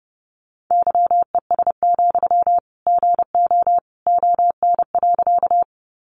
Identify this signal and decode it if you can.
Morse code, heard as a keyed tone, about 30 wpm, 700 Hz. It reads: YES, GO ON.
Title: Sentences from Top 500 Words 30wpm